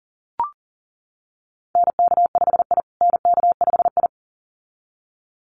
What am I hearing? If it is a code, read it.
Morse code, heard as a keyed tone, about 40 wpm, 700 Hz. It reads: NK5I NK5I
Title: USA Call Signs Repeated 2x 40wpm